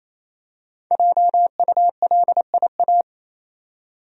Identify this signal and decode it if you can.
Morse code, heard as a keyed tone, about 28 wpm, 700 Hz. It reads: JULIA
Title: USA Common Names - Max 5 Letters - 28wpm